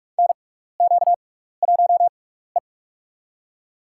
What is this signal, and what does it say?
Morse code, heard as a keyed tone, about 44 wpm, spaced as for 15 wpm, 700 Hz. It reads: NQ1E
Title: USA Call Signs - 6x Character Spacing - 45wpm